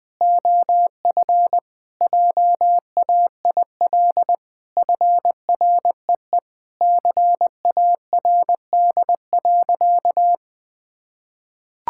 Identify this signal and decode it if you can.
Morse code, heard as a keyed tone, about 20 wpm, 700 Hz. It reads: OF JAIL FREE CARD.
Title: Sentences with 7-Word Phrases - 20wpm